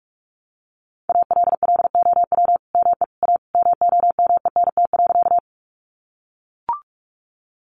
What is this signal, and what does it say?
Morse code, heard as a keyed tone, about 44 wpm, 700 Hz. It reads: ALLOW ME A MOMENT.
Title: Sentences from Top 1000 Words 45wpm